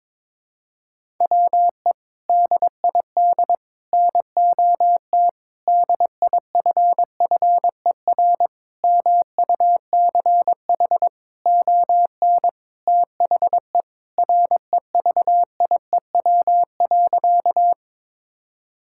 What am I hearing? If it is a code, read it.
Morse code, heard as a keyed tone, about 22 wpm, 700 Hz. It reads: WE DID NOT DIFFER MUCH ON THE REVIEW.
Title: Sentences from Top 1000 Words - Review Part 6 - 22wpm